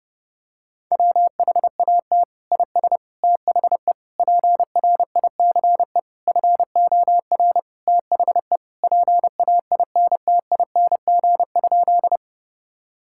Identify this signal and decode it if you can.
Morse code, heard as a keyed tone, about 32 wpm, 700 Hz. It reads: WHAT IS THE PRICE FOR THE PAINTING?
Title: Sentences from Top 1000 Words - Review Part 6 - 30wpm